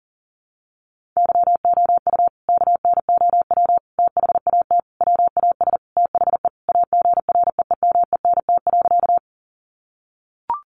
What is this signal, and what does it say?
Morse code, heard as a keyed tone, about 37 wpm, 700 Hz. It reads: YOU KNOW THAT WAS THE AGREEMENT.
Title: Sentences from Top 1500 Words 40wpm